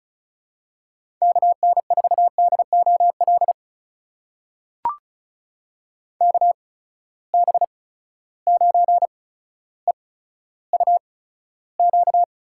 Morse code, heard as a keyed tone, about 36 wpm, spaced as for 15 wpm, 700 Hz. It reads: KN4DOL KB9EUQ
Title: USA Call Signs - 6 Characters - 8x Character Spacing - 35wpm